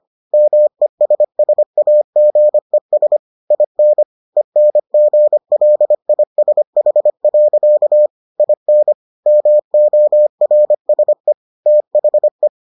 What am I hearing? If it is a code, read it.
Morse code, heard as a keyed tone, about 25 wpm, 600 Hz. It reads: MESSAGES IN ENGLISH. IN MORSE THE